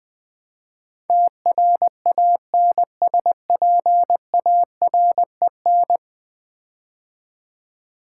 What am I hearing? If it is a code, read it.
Morse code, heard as a keyed tone, about 20 wpm, 700 Hz. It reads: TRANSPAREN
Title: Build A Word - 5 Letters At A Time - 12-Letter Words - 20wpm